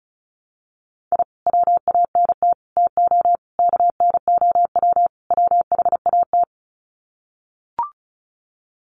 Morse code, heard as a keyed tone, about 34 wpm, 700 Hz. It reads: I WANT TO KNOW WHAT
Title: Sets of 5 Words Encore2 35wpm